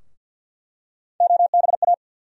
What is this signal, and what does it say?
Morse code, heard as a keyed tone, about 50 wpm, 700 Hz. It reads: OBA
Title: USA Call Sign Postfixes - 3 Letters 50wpm